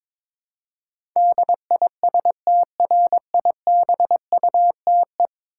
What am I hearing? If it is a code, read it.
Morse code, heard as a keyed tone, about 22 wpm, 700 Hz. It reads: DISTRIBUTE
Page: morse-code-ninja-practice-22wpm